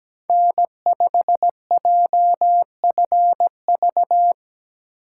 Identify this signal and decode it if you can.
Morse code, heard as a keyed tone, about 17 wpm, 700 Hz. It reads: N5JFV
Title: USA Call Signs - 5 Characters - 4x Character Spacing - 17wpm